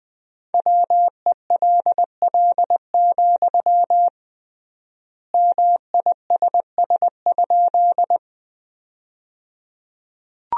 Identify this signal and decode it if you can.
Morse code, heard as a keyed tone, about 20 wpm, 700 Hz. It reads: WELL, MISS?
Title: 3x Word-Spacing - Sentences from Top 500 Words - 20wpm